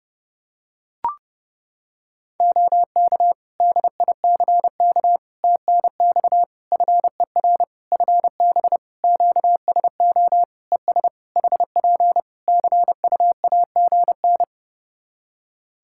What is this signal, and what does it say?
Morse code, heard as a keyed tone, about 30 wpm, 700 Hz. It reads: OK DICK TNX FER FB QSO ES HP CUAGN
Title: Practice QSO - Exchange 4 - Ending - 30wpm